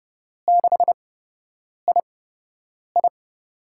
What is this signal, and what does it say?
Morse code, heard as a keyed tone, about 30 wpm, spaced as for 8 wpm, 700 Hz. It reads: 6II
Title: USA Call Signs - 4 Characters - 8x Character Spacing - 30wpm